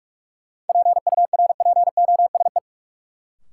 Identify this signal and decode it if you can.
Morse code, heard as a keyed tone, about 43 wpm, 700 Hz. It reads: PURPOSE
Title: Sentences from Top 800 Words - New Word Drill - 45wpm